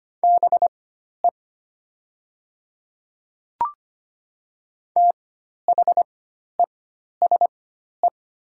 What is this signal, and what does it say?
Morse code, heard as a keyed tone, about 25 wpm, spaced as for 12 wpm, 700 Hz. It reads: BE THESE
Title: 4x Character Spacing - Top 100 Words - 25wpm